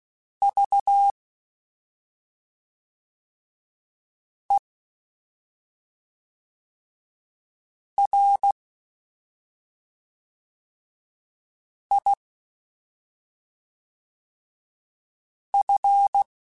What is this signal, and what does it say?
Morse code, heard as a keyed tone, about 16 wpm, 800 Hz. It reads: V E R I F
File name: Special INGs / Special INGs-mp3s / SlowToFaster ing @5